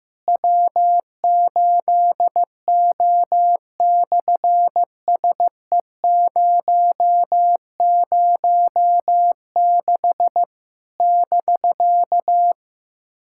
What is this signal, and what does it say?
Morse code, heard as a keyed tone, about 15 wpm, 700 Hz. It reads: W8O/SE006 <BK>
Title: SOTA QSO Practice 15wpm